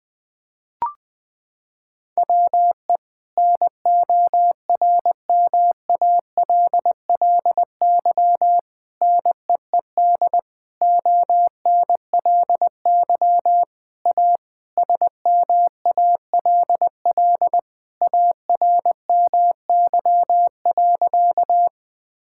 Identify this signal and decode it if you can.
Morse code, heard as a keyed tone, about 20 wpm, 700 Hz. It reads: WE NORMALLY NEED ONLY A SMALL ARMY.
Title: Sentences from Top 1000 Words - Review Part 6 - 20wpm